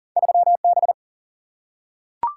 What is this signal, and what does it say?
Morse code, heard as a keyed tone, about 42 wpm, 700 Hz. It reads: 3 B
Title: Single-Letter Single-Number 40wpm